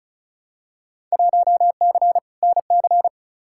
Morse code, heard as a keyed tone, about 36 wpm, 700 Hz. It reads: 1C NC
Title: Field Day Exchanges - 35wpm